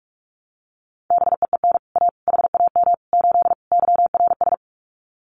Morse code, heard as a keyed tone, about 44 wpm, 700 Hz. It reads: BEEN A HAM 8 YRS
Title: Common QSO Phrases Encore 45wpm